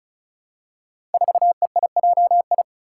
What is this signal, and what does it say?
Morse code, heard as a keyed tone, about 36 wpm, 700 Hz. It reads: 4EIJI
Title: Five Challenging Letter-Number Combinations 35wpm